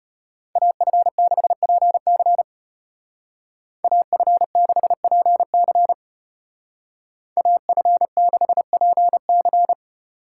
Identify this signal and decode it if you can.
Morse code, heard as a keyed tone, about 33 wpm, 700 Hz. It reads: AF6PC AF6PC AF6PC